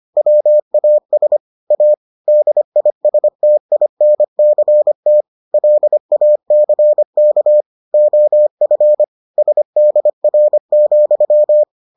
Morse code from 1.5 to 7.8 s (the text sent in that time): A DISTINCT LACK